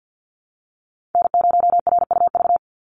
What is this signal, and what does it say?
Morse code, heard as a keyed tone, about 50 wpm, 700 Hz. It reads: N0RUV
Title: USA Call Signs 50wpm